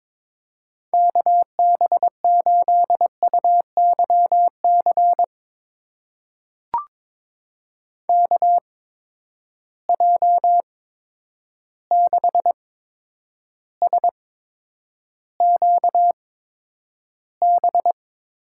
Morse code, heard as a keyed tone, about 22 wpm, spaced as for 9 wpm, 700 Hz. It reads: KB8UYC KJ6SQB
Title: USA Call Signs - 6 Characters - 8x Character Spacing - 22wpm